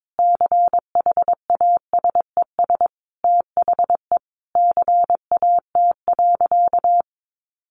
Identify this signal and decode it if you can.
Morse code, heard as a keyed tone, about 22 wpm, 700 Hz. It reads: CHASES THE CAT.
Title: Easy Phrases 22wpm